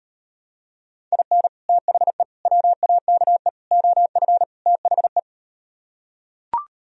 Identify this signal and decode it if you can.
Morse code, heard as a keyed tone, about 39 wpm, 700 Hz. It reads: IN THE WAKE OF THE